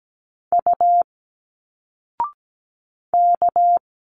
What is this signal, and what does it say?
Morse code, heard as a keyed tone, about 17 wpm, 700 Hz. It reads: U K
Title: Single Letters 17wpm